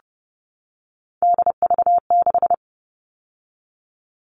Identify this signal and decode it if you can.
Morse code, heard as a keyed tone, about 29 wpm, 700 Hz. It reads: DV6
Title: All Cut Numbers - 3-Digit 30wpm